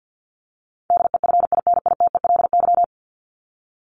Morse code, heard as a keyed tone, about 48 wpm, 700 Hz. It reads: DEFINITELY
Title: Top 1751-2000 Words 50wpm